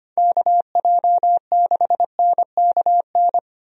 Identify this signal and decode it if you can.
Morse code, heard as a keyed tone, about 25 wpm, 700 Hz. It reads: KJ6NKN
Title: USA Call Signs - 6 Characters - 25wpm